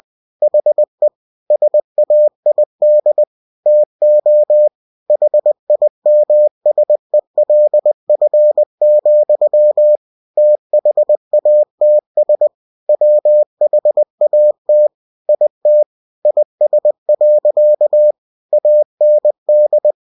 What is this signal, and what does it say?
Morse code, heard as a keyed tone, about 20 wpm, 600 Hz. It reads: HE SAID TO HIMSELF, THATS WHAT IT IS. AND